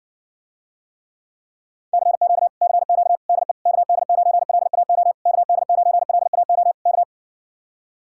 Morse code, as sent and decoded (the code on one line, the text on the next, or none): -.-. --.- / -.-. --.- / -.. / -.- -.. ----. .-.. .- --- / -.- -.. ----. .-.. .- --- / -.-
CQ CQ DE KD9LAO KD9LAO K